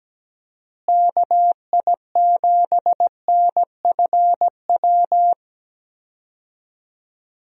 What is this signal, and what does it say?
Morse code, heard as a keyed tone, about 17 wpm, 700 Hz. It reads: KI7NFW
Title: USA Call Signs - 6 Characters - Repeated 2x - 17wpm